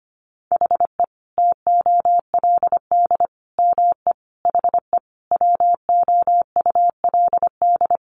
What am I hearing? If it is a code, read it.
Morse code, heard as a keyed tone, about 25 wpm, 700 Hz. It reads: HE TOLD ME HE WOULD
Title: Sentences from Top 1300 Words 25wpm